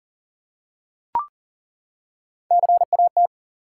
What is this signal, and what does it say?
Morse code, heard as a keyed tone, about 40 wpm, 700 Hz. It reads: CAT
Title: Three Letter Words - 40wpm